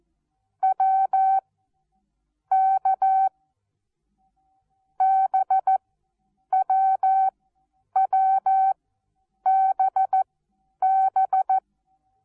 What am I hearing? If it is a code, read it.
Morse code, heard as a keyed tone, about 14 wpm, spaced as for 9 wpm, 770 Hz. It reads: W K BWWBB
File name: Onlinelehrgang / morsen / DJ4UF Morsen19